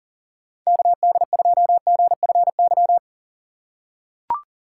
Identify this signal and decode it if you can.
Morse code, heard as a keyed tone, about 37 wpm, 700 Hz. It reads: KD2GFY
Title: USA Call Signs - 6 Characters - 3x Character Spacing - 40wpm